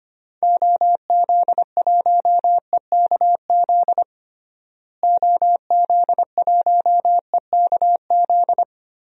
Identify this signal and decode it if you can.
Morse code, heard as a keyed tone, about 25 wpm, 700 Hz. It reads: OZ1EKZ OZ1EKZ
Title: DXCC-100 Call Signs - 6 Characters - Repeated 3x - 25wpm